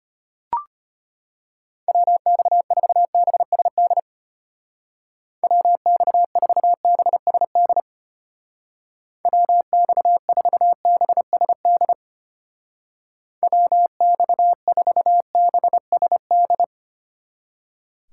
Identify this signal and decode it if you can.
Morse code, heard as a keyed tone, about 32 wpm, 700 Hz. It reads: WX4BSD WX4BSD WX4BSD WX4BSD